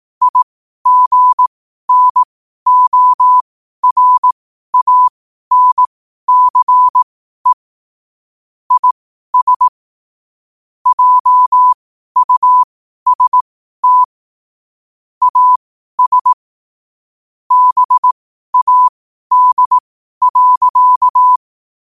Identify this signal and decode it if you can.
Morse code, heard as a keyed tone, about 18 wpm, spaced as for 12 wpm, 1000 Hz. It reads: IGNORANCE IS JUST AS BAD.
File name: qotdfiles / 12wpm